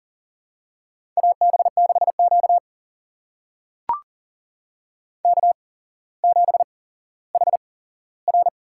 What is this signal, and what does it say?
Morse code, heard as a keyed tone, about 40 wpm, spaced as for 17 wpm, 700 Hz. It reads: AB6Q K7HR
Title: USA Call Signs - 8x Character Spacing - 40wpm